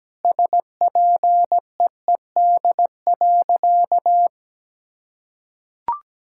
Morse code, heard as a keyed tone, about 17 wpm, 700 Hz. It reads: SPEED.
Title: Sentences from Top 1000 Words - Review Part 5 - 17wpm